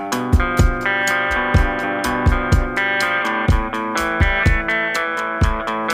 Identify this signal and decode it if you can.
Emergency Alert System (SAME) tones (header bursts), heard as harsh buzzing bursts. It is heard from 0.8 to 5.3 s.